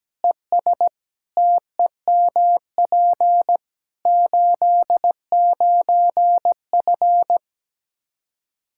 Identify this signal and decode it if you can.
Morse code, heard as a keyed tone, about 17 wpm, 700 Hz. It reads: ES TEMP 89F
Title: Practice QSO - Exchange 2 - Info - 17wpm